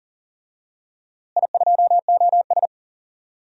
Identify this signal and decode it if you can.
Morse code, heard as a keyed tone, about 40 wpm, 700 Hz. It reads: I2OS
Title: DXCC-100 Call Signs - 4 Characters - Repeated 3x - 40wpm